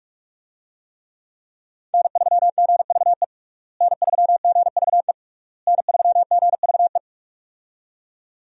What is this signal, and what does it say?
Morse code, heard as a keyed tone, about 46 wpm, 700 Hz. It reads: N3GVE N3GVE N3GVE
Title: USA Call Signs - 5 Characters - Repeated 3x - 45wpm